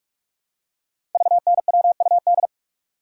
Morse code, heard as a keyed tone, about 44 wpm, 690 Hz. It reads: VNWUD
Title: Code Groups - 5 Letters - 45wpm